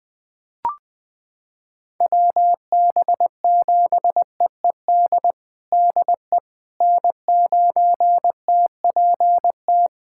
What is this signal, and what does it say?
Morse code, heard as a keyed tone, about 20 wpm, 700 Hz. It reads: WB7EED DE N9TPT
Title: USA Callsign DE Callsign 20wpm